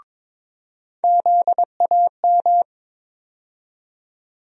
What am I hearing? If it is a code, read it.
Morse code, heard as a keyed tone, about 22 wpm, 700 Hz. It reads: ZAM
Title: USA Call Sign Postfixes - 3 Letters 22wpm